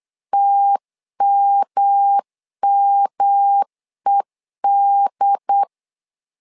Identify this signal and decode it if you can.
Morse code, heard as a keyed tone, about 8 wpm, 800 Hz. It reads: TMMED
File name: LEZIONI / 3 MWRO / 3 M 8wpm 2300